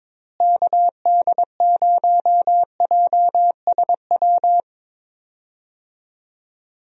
Morse code, heard as a keyed tone, about 22 wpm, 700 Hz. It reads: KD0JSW
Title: USA Call Signs - 6 Characters - Repeated 3x - 22wpm